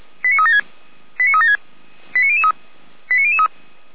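Five-tone selective call (ZVEI), heard as quick runs of five tones, tones of about 70 ms. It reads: 89376 89376 89003 89003
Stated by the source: ZVEI five-tone call received by SDR.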